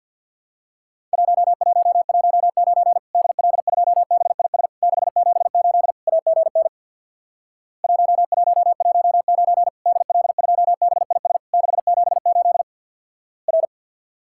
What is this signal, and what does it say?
Morse code, heard as a keyed tone, about 50 wpm, 700 Hz. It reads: 1119 DL2BIS 678 AGN 1119 DL2BIS 678 R
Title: Worked All Europe DX Contest - QTC Exchange with Repeats - 50wpm